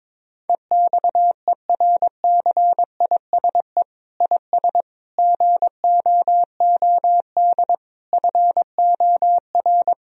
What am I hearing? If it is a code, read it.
Morse code, heard as a keyed tone, about 22 wpm, 700 Hz. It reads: EXERCISE IS GOOD FOR
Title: Sentences from Top 1000 Words - Review Part 4 - 22wpm